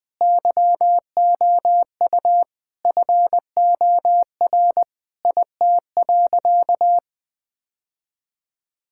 Morse code, heard as a keyed tone, about 21 wpm, 700 Hz. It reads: YOU FOR IT.
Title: Sentences from Top 1000 Words - Review Part 6 - 20wpm